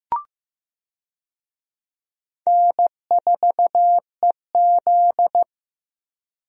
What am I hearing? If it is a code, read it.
Morse code, heard as a keyed tone, about 15 wpm, 700 Hz. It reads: N4EZ
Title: USA Call Signs - 4 Characters - Repeated 3x - 15wpm